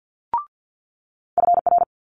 Morse code, heard as a keyed tone, about 50 wpm, 700 Hz. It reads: FR
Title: Multi-letter Phonemes Quick Fire 50wpm